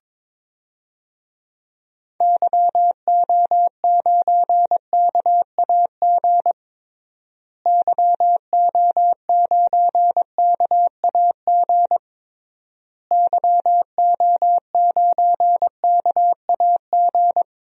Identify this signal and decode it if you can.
Morse code, heard as a keyed tone, about 22 wpm, 700 Hz. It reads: YO9KAG YO9KAG YO9KAG
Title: DXCC-100 Call Signs - 6 Characters - Repeated 3x - 22wpm